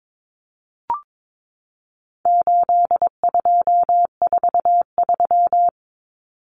Morse code, heard as a keyed tone, about 22 wpm, 700 Hz. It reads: 8243
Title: Common Cut Numbers - 4-Digit 22wpm